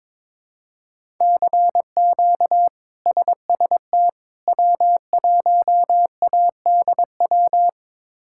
Morse code, heard as a keyed tone, about 22 wpm, 700 Hz. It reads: CQ SST W1ADW
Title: K1USN SST QSO Practice - 22wpm